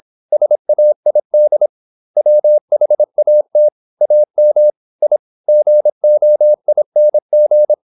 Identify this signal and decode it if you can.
Morse code, heard as a keyed tone, about 26 wpm, 600 Hz. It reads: SAID WHAT AM I GOING
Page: Never 26wpm | CW med Gnister